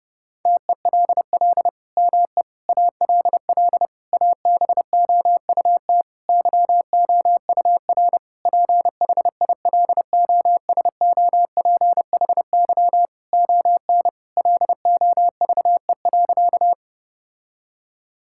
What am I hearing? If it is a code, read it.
Morse code, heard as a keyed tone, about 30 wpm, 700 Hz. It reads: TELL ME ALL ABOUT YOUR PHILOSOPHY ON LOVE.
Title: Sentences from Top 1000 Words - Review Part 6 - 30wpm